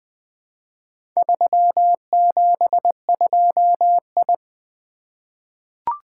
Morse code, heard as a keyed tone, about 20 wpm, 700 Hz. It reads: 372I
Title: Four Letter-Number Combinations - 20wpm